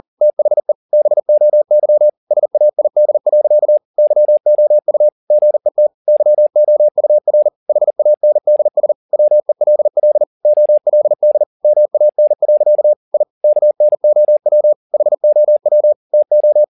Morse code, heard as a keyed tone, about 40 wpm, 600 Hz. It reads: THE BOY SAID. YOU GET YOUR HANDS WELL OLD MAN. I KNOW HOW TO
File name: the old man and the sea 2 40wpm